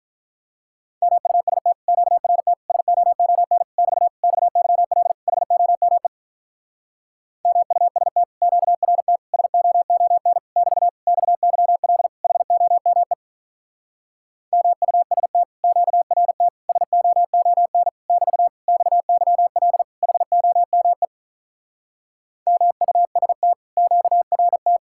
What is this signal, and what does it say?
Morse code, heard as a keyed tone, about 48 wpm, 700 Hz. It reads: MUST QRT SOON <BT> XYL HOME MUST QRT SOON <BT> XYL HOME MUST QRT SOON <BT> XYL HOME MUST QRT